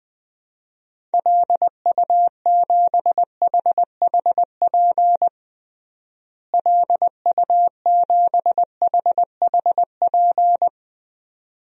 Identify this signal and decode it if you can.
Morse code, heard as a keyed tone, about 20 wpm, 700 Hz. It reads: LU7HHP LU7HHP
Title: DXCC-100 Call Signs - 6 Characters - Repeated 2x - 20wpm